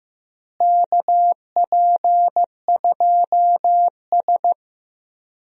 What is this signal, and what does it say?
Morse code, heard as a keyed tone, about 15 wpm, 700 Hz. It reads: KP2S
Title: DXCC-100 Call Signs - 4 Characters - Repeated 3x - 15wpm